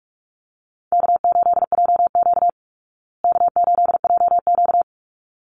Morse code, heard as a keyed tone, about 44 wpm, 700 Hz. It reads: K8JQ K8JQ
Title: Call Signs Repeated 3x - 45wpm